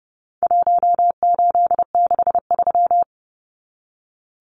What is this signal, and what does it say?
Morse code, heard as a keyed tone, about 30 wpm, 700 Hz. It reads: 1863
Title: Numbers - Contest Style - 4-Digit 30wpm